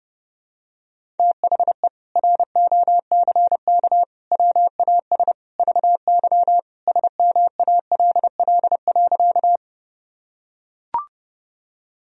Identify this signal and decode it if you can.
Morse code, heard as a keyed tone, about 30 wpm, 700 Hz. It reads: THE ROCK WAS VY SMALL.
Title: Sentences from Top 400 Words and Abbreviations 30wpm